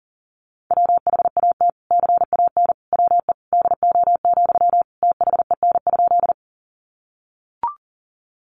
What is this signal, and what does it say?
Morse code, heard as a keyed tone, about 37 wpm, 700 Hz. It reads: WHAT CAN WE DO, THEN?
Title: Sentences from Top 100 Words 40wpm